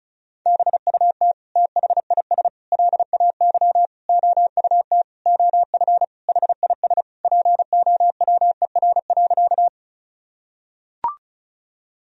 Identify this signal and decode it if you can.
Morse code, heard as a keyed tone, about 34 wpm, 700 Hz. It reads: BUT THIS LAY OUT OF HIS POWER.
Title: Sentences from Top 1000 Words - Review Part 2 - 35wpm